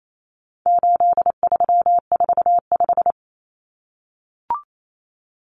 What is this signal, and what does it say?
Morse code, heard as a keyed tone, about 28 wpm, 700 Hz. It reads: 8345
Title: Common Cut Numbers - 4-Digit 28wpm